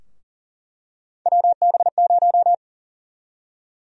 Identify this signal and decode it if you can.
Morse code, heard as a keyed tone, about 40 wpm, 700 Hz. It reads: WB0
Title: USA Call Sign Prefixes - 40wpm